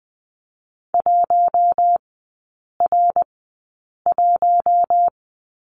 Morse code, heard as a keyed tone, about 20 wpm, spaced as for 8 wpm, 700 Hz. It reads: 1R1
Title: 053 Lesson 10e - Letter r - Three Characters 20wpm